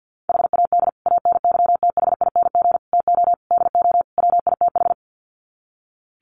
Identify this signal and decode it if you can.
Morse code, heard as a keyed tone, about 50 wpm, 700 Hz. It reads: HAD ANYTHING TO DO WITH
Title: Sets of 5 Words Encore 50wpm